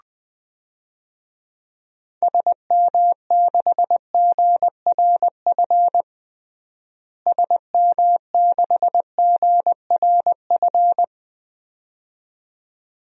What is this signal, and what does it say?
Morse code, heard as a keyed tone, about 20 wpm, 700 Hz. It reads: SM6GRF SM6GRF
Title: DXCC-100 Call Signs - 6 Characters - Repeated 2x - 20wpm